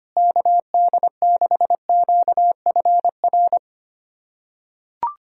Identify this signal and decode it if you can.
Morse code, heard as a keyed tone, about 25 wpm, 700 Hz. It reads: KD6QFR
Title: USA Callsign DE Callsign 25wpm